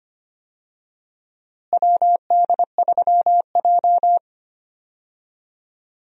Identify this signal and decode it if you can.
Morse code, heard as a keyed tone, about 25 wpm, 700 Hz. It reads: WD3J
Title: USA Call Signs - 4 Characters - Repeated 2x - 25wpm